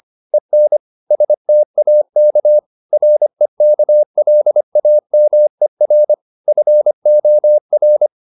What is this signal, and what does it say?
Morse code, heard as a keyed tone, about 25 wpm, 600 Hz. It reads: EN STAK REKLAMER FOR
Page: Ferie 25 wpm